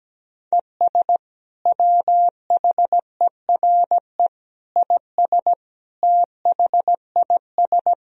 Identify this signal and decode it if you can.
Morse code, heard as a keyed tone, about 17 wpm, 700 Hz. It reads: ES WHERE IS THIS
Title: Sentences from Top 200 Words and Abbreviations 17wpm